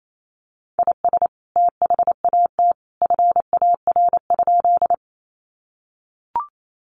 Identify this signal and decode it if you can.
Morse code, heard as a keyed tone, about 28 wpm, 700 Hz. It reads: IS THAT FAR?
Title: Sentences from Top 300 Words 28wpm